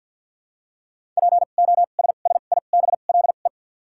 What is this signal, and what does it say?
Morse code, heard as a keyed tone, about 52 wpm, spaced as for 37 wpm, 700 Hz. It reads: POSSIBLE
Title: 2x Character Spacing - Top 500 Words - 50wpm